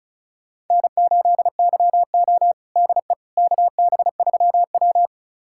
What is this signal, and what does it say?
Morse code, heard as a keyed tone, about 36 wpm, 700 Hz. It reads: N8YO DE KB3W
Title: USA Callsign DE Callsign - 4 Characters - 35wpm